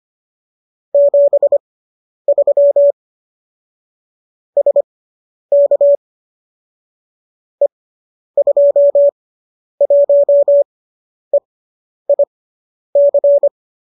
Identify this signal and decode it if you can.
Morse code, heard as a keyed tone, about 25 wpm, spaced as for 10 wpm, 570 Hz. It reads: 73 SK E21EIC
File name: qso005 10